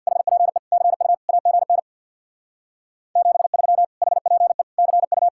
Morse code, heard as a keyed tone, about 63 wpm, 700 Hz. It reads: HPE CU AGN 73 HPE CU